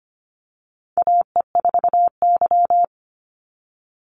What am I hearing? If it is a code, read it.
Morse code, heard as a keyed tone, about 25 wpm, 700 Hz. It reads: AE4Y
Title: NA SOTA Chaser Callsigns 25wpm